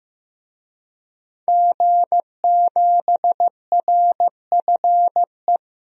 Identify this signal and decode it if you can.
Morse code, heard as a keyed tone, about 16 wpm, 700 Hz. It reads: G7RFE
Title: DXCC-100 Call Signs - 5 Characters - 15wpm